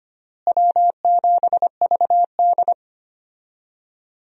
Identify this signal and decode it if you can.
Morse code, heard as a keyed tone, about 25 wpm, 700 Hz. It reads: W7VD
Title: USA Call Signs - 4 Characters - Repeated 2x - 25wpm